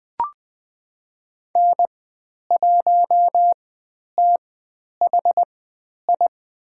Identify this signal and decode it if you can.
Morse code, heard as a keyed tone, about 20 wpm, spaced as for 10 wpm, 700 Hz. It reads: N1THI
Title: 095 Lesson 16g - Letter c - Call Signs 20wpm